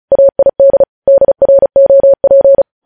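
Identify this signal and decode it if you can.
Morse code, heard as a keyed tone, about 36 wpm, 550 Hz. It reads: AID DROP
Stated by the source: Morse code news headlines sent at 35.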